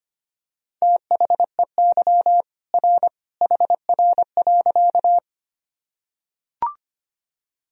Morse code, heard as a keyed tone, about 25 wpm, 700 Hz. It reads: THEY R HR.
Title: Sentences from Top 200 Words and Abbreviations 25wpm